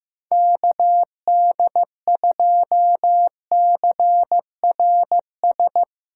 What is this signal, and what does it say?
Morse code, heard as a keyed tone, about 16 wpm, 700 Hz. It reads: KD2CRS
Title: Speed Racing - USA 2x3 Call Signs 15wpm